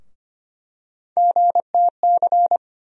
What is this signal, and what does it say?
Morse code, heard as a keyed tone, about 25 wpm, 700 Hz. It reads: GTC
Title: USA Call Sign Postfixes - 3 Letters 25wpm